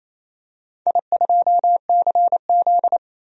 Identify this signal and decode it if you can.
Morse code, heard as a keyed tone, about 27 wpm, 700 Hz. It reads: I2CZ
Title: DXCC-100 Call Signs - 4 Characters - 28wpm